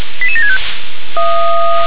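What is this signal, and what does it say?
Five-tone selective call (ZVEI), heard as quick runs of five tones, tones of about 70 ms. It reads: 88765
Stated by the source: Five-tone ZVEI selective call received on an SDR.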